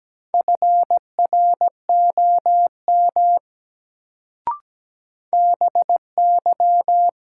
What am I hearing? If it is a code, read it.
Morse code, heard as a keyed tone, about 17 wpm, 700 Hz. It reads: FROM BY
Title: Sentences from Top 100 Words - New Word Drill - 17wpm